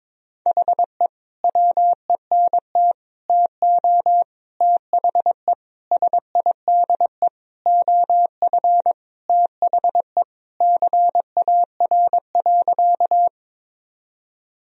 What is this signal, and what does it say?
Morse code, heard as a keyed tone, about 22 wpm, 700 Hz. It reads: HE WENT TO THE SIDE OF THE CAR.
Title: Sentences from Top 1000 Words - Review Part 1 - 22wpm